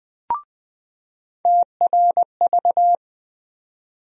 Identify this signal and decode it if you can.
Morse code, heard as a keyed tone, about 20 wpm, 700 Hz. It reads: TRV
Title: USA Call Sign Postfixes - 3 Letters 20wpm